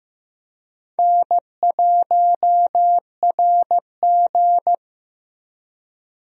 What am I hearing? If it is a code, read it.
Morse code, heard as a keyed tone, about 16 wpm, 700 Hz. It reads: N1RG
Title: USA Call Signs - 4 Characters - Repeated 2x - 15wpm